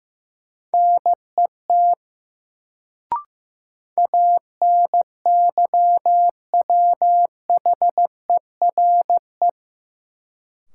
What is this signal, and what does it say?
Morse code, heard as a keyed tone, about 15 wpm, 700 Hz. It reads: NET ANYWHERE
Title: Sentences from Top 1400 Words - New Word Drill - 15wpm